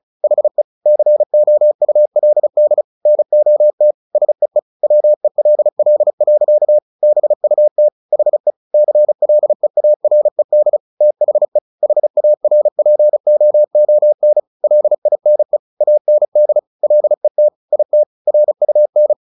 Morse code, heard as a keyed tone, about 35 wpm, 600 Hz. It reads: HE COULD NOT SEE WELL. BUT HE CLEARED THE HARPOON LINE AND LET IT RUN
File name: the old man and the sea 2 35wpm